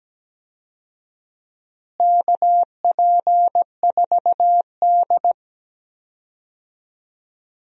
Morse code, heard as a keyed tone, about 17 wpm, 700 Hz. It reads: KP4D
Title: DXCC-100 Call Signs - 4 Characters - Repeated 2x - 17wpm